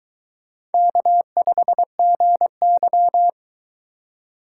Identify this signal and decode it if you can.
Morse code, heard as a keyed tone, about 22 wpm, 700 Hz. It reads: K5GY